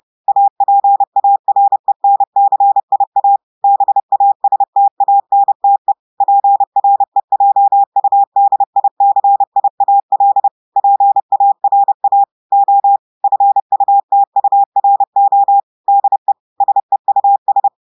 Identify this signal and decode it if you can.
Morse code, heard as a keyed tone, about 30 wpm, 800 Hz. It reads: APARENCIA BASTANTE PREJUDICIAL PARA O FUTURO DE SEUS
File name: audios / portugais / portugais texte en clair 1 vitesse 30 mots minutes